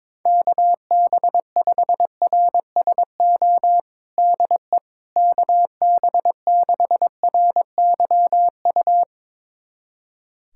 Morse code, heard as a keyed tone, about 21 wpm, 700 Hz. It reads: KB5RSO DE KB6RYU
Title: USA Callsign DE Callsign - 6 Characters - 22wpm